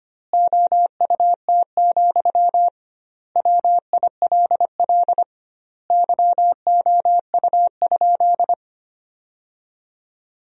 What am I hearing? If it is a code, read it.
Morse code, heard as a keyed tone, about 25 wpm, 700 Hz. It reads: OUT, WILL YOU?
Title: 2x Word-Spacing - Sentences from Top 300 Words - 25wpm